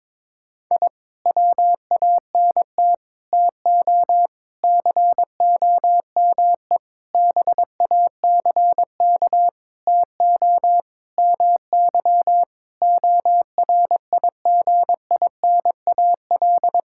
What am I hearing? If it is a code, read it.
Morse code, heard as a keyed tone, about 22 wpm, 700 Hz. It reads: I WANT TO COME BACK TO MY ORIGINAL
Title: Sentences from Top 1000 Words - Review Part 6 - 22wpm